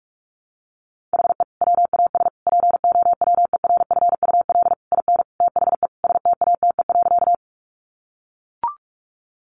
Morse code, heard as a keyed tone, about 44 wpm, 700 Hz. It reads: HE WAS POWERFUL IN THE STATE.
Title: Sentences from Top 1300 Words 45wpm